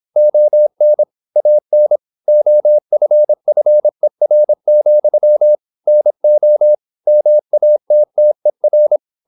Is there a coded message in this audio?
Yes: Morse code from 0.2 to 9.0 s, heard as a keyed tone, about 26 wpm, 600 Hz. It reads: ON AN OFFER, NO MATTER